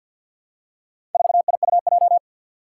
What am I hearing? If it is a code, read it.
Morse code, heard as a keyed tone, about 51 wpm, 700 Hz. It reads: 4IUJ
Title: Four Challenging Letter-Number Combinations 50wpm